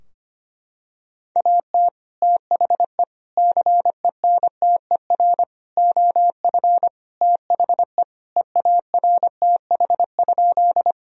Morse code, heard as a keyed tone, about 25 wpm, 700 Hz. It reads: AT THE CENTER OF THE EARTH?
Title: Sentences from Top 1000 Words - Review Part 2 - 25wpm